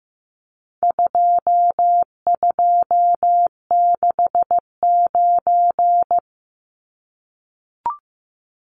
Morse code, heard as a keyed tone, about 15 wpm, 700 Hz. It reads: 2269
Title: Numbers - 4-Digit 15wpm